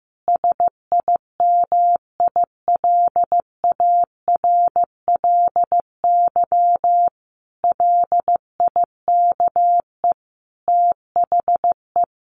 Morse code, heard as a keyed tone, about 15 wpm, 700 Hz. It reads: SIMILARLY LIKE THE
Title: Sentences from Top 1400 Words 15wpm